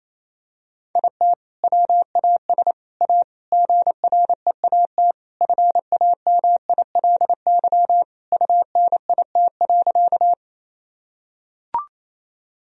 Morse code, heard as a keyed tone, about 27 wpm, 700 Hz. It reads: IT WAS A GREAT FAMILY UNIT.
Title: Sentences from Top 1000 Words - Review Part 2 - 28wpm